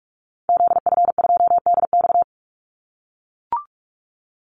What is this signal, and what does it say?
Morse code, heard as a keyed tone, about 44 wpm, 700 Hz. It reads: ZF2DX
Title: Call Signs - 45wpm